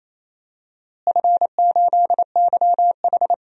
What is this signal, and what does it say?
Morse code, heard as a keyed tone, about 28 wpm, 700 Hz. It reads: F8YH
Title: DXCC-100 Call Signs - 4 Characters - Repeated 3x - 28wpm